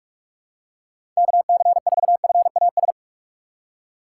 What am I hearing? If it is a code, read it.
Morse code, heard as a keyed tone, about 46 wpm, 700 Hz. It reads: KC4FAS
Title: USA Call Signs - 4x Character Spacing - 45wpm